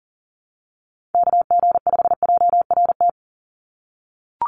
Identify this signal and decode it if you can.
Morse code, heard as a keyed tone, about 40 wpm, 700 Hz. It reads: KG5JRT
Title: USA Call Signs 40wpm